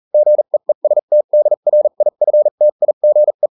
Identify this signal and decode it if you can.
Morse code, heard as a keyed tone, about 40 wpm, spaced as for 35 wpm, 600 Hz. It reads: GEESTDRIFTIGE